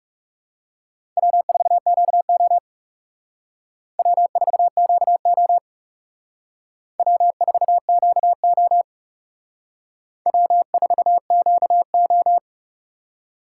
Morse code, heard as a keyed tone, about 36 wpm, 700 Hz. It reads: W4QO W4QO W4QO W4QO
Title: Speed Racing - USA 1x2 Call Signs 30wpm